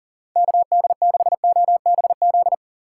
Morse code, heard as a keyed tone, about 40 wpm, 700 Hz. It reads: KD6OBZ
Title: USA Call Signs - 6 Characters - 4x Character Spacing - 40wpm